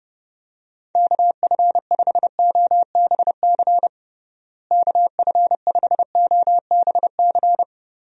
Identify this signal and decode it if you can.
Morse code, heard as a keyed tone, about 32 wpm, 700 Hz. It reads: KF5OBC KF5OBC
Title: USA Call Signs - 6 Characters - Repeated 2x - 30wpm